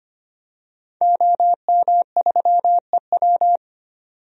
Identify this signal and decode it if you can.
Morse code, heard as a keyed tone, about 25 wpm, 700 Hz. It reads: OM3EW
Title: DXCC-100 Call Signs - 5 Characters - 25wpm